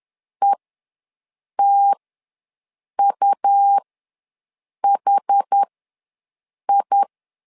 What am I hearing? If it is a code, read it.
Morse code, heard as a keyed tone, about 10 wpm, spaced as for 6 wpm, 800 Hz. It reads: ETUHI
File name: LEZIONI / 2 UHDN / 2 UHD ws 10wpm 2058